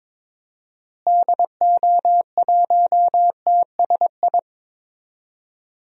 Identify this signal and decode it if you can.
Morse code, heard as a keyed tone, about 22 wpm, 700 Hz. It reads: DO1TSI
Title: DXCC-100 Call Signs - 6 Characters - Repeated 3x - 22wpm